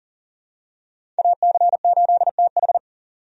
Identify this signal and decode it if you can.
Morse code, heard as a keyed tone, about 40 wpm, 700 Hz. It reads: AC8TH
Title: USA Call Signs - 5 Characters - 40wpm